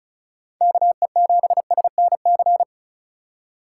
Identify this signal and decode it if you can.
Morse code, heard as a keyed tone, about 34 wpm, 700 Hz. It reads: KE7SNC
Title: USA Call Signs - 6 Characters - 4x Character Spacing - 35wpm